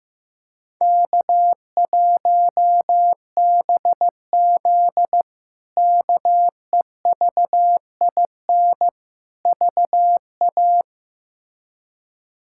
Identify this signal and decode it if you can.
Morse code, heard as a keyed tone, about 15 wpm, 700 Hz. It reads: K1BZ KEVIN VA
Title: K1USN SST QSO Practice - 15wpm